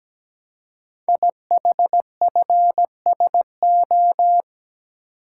Morse code, heard as a keyed tone, about 17 wpm, 700 Hz. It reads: IHFSO
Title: Code Groups - 5 Letters - 17wpm